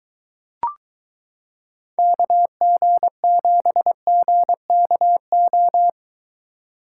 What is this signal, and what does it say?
Morse code, heard as a keyed tone, about 23 wpm, 700 Hz. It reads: KG7GKO